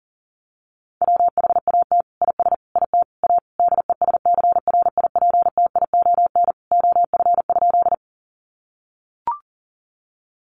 Morse code, heard as a keyed tone, about 37 wpm, 700 Hz. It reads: WHAT IS IT A DESCRIPTION OF?
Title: Sentences from Top 1300 Words 40wpm